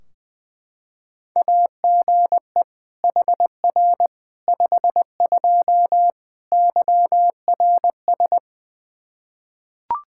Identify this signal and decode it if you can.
Morse code, heard as a keyed tone, about 20 wpm, 700 Hz. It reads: AGE HR 52 YRS
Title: Practice QSO - Exchange 3 - More Info - 20wpm